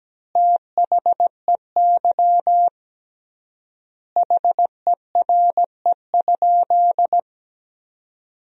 Morse code, heard as a keyed tone, about 17 wpm, 700 Hz. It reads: THEY HERE?
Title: 3x Word-Spacing - Sentences from Top 200 Words - 17wpm